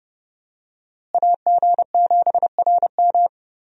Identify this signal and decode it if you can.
Morse code, heard as a keyed tone, about 30 wpm, 700 Hz. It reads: AG7RM
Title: USA Call Signs - 5 Characters - Repeated 3x - 30wpm